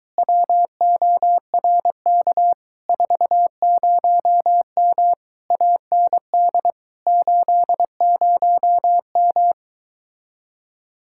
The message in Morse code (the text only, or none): WORK 40M AND 80M